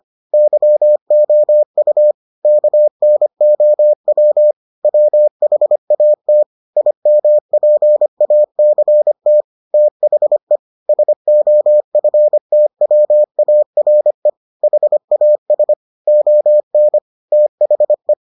Morse code, heard as a keyed tone, about 25 wpm, 600 Hz. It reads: YOU KNOW WHAT IMPACT THE SOFTWARE HAS ON THE